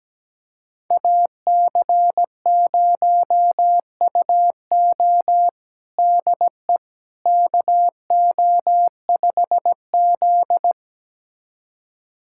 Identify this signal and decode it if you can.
Morse code, heard as a keyed tone, about 17 wpm, 700 Hz. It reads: AC0UO DE KO5Z
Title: USA Callsign DE Callsign 17wpm